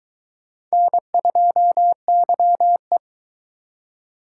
Morse code, heard as a keyed tone, about 23 wpm, 700 Hz. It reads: N2YE